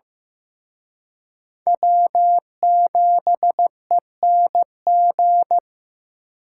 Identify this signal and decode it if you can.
Morse code, heard as a keyed tone, about 18 wpm, 700 Hz. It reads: W7ENG
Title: Speed Racing - USA 1x3 Call Signs 15wpm